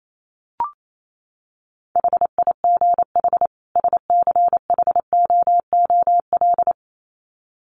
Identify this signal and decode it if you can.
Morse code, heard as a keyed tone, about 28 wpm, 700 Hz. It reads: HIGH SCHOOL
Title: Sets of 2 Words 28wpm